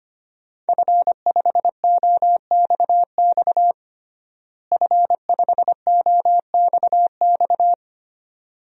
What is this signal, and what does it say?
Morse code, heard as a keyed tone, about 25 wpm, 700 Hz. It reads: F5OXX F5OXX
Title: DXCC-100 Call Signs - 5 Characters - Repeated 2x - 25wpm